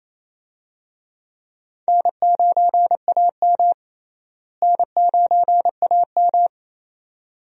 Morse code, heard as a keyed tone, about 28 wpm, 700 Hz. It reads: N9AM N9AM
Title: USA Call Signs - 4 Characters - Repeated 2x - 28wpm